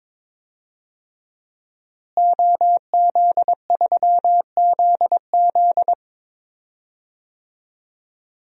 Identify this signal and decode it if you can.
Morse code, heard as a keyed tone, about 22 wpm, 700 Hz. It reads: OZ3ZZ
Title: DXCC-100 Call Signs - 5 Characters - Repeated 3x - 22wpm